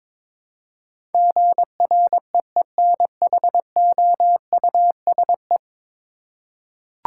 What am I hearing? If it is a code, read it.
Morse code, heard as a keyed tone, about 22 wpm, 700 Hz. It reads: GREENHOUSE
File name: practice / 22wpm / 10 Letter Words - First 3 Letters Given - 22wpm